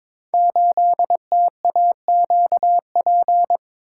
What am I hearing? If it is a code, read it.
Morse code, heard as a keyed tone, about 22 wpm, 700 Hz. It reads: 8TAQP
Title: Code Groups - 5 Letter:Numbers - 22wpm